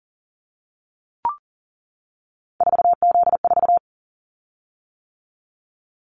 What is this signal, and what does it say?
Morse code, heard as a keyed tone, about 40 wpm, 700 Hz. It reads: 4Z4
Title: Callsign Prefixes for Top 50 DXCC - 40wpm